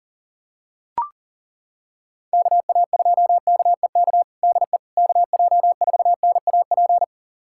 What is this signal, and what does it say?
Morse code, heard as a keyed tone, about 40 wpm, 700 Hz. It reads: KA2KEK DE KJ4NAP
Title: USA Callsign DE Callsign - 6 Characters - 40wpm